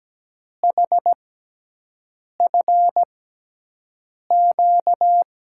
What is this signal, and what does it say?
Morse code, heard as a keyed tone, about 17 wpm, spaced as for 6 wpm, 700 Hz. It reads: HFQ
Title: USA Call Signs - 6 Characters - 6x Character Spacing - 17wpm